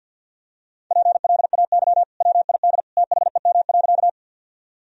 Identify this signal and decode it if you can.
Morse code, heard as a keyed tone, about 51 wpm, 700 Hz. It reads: PLAY WID THEM.
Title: Sentences from Top 200 Words and Abbreviations 50wpm